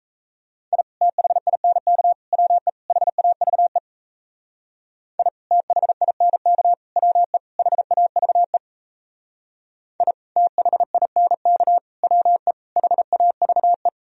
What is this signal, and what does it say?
Morse code, heard as a keyed tone, about 34 wpm, 700 Hz. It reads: I THINK WE HAVE I THINK WE HAVE I THINK WE HAVE
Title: Speed-Racing - Sets of 4 Words 28wpm